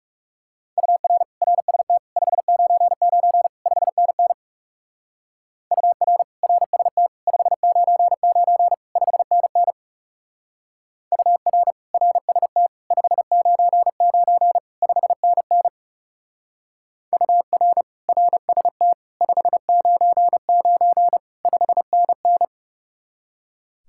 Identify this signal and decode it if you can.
Morse code, heard as a keyed tone, about 36 wpm, 700 Hz. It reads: UR RST 599 5NN UR RST 599 5NN UR RST 599 5NN UR RST 599 5NN
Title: Speed-Racing Common QSO Phrases - 30wpm